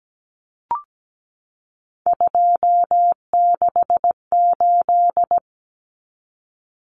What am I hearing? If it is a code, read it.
Morse code, heard as a keyed tone, about 17 wpm, 700 Hz. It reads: 268
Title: Common Cut Numbers - 3-Digit 17wpm